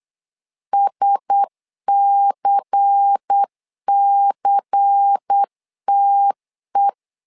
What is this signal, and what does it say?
Morse code, heard as a keyed tone, about 8 wpm, 800 Hz. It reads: SCCTE
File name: LEZIONI / 5 LBVC / 5 LBVC 8wpm 2935